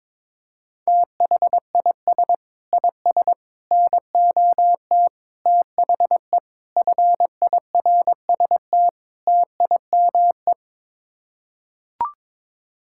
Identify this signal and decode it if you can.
Morse code, heard as a keyed tone, about 22 wpm, 700 Hz. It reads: THIS IS NOT THE FIRST TIME
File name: practice / 22wpm / Sets of 6 Words - 22wpm